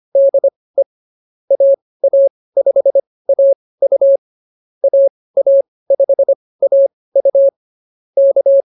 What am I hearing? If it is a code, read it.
Morse code, heard as a keyed tone, about 24 wpm, spaced as for 18 wpm, 560 Hz. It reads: DE AA5AU AA5AU K
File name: QSO104 18